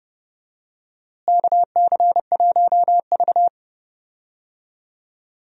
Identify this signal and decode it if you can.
Morse code, heard as a keyed tone, about 32 wpm, 700 Hz. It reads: KC1V
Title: USA Call Signs - 4 Characters - Repeated 2x - 30wpm